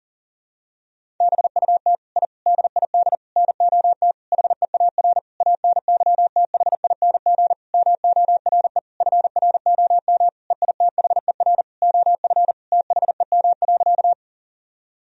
Morse code, heard as a keyed tone, about 40 wpm, 700 Hz. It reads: BUT I DID NOT HEAR ANYTHING MORE FROM EITHER OF THEM.
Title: Sentences from Top 1000 Words - Review Part 4 - 40wpm